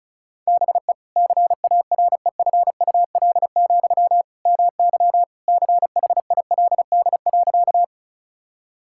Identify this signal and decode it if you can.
Morse code, heard as a keyed tone, about 34 wpm, 700 Hz. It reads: BE CAREFUL, MY CHILD.
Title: Sentences from Top 1600 Words 35wpm